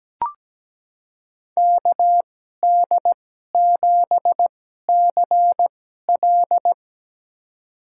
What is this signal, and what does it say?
Morse code, heard as a keyed tone, about 17 wpm, spaced as for 12 wpm, 700 Hz. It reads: KD7CL
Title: USA Call Signs - 5 Characters - 2x Character Spacing - 17wpm